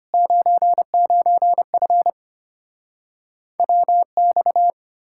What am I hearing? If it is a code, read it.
Morse code, heard as a keyed tone, about 33 wpm, 700 Hz. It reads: 99F WX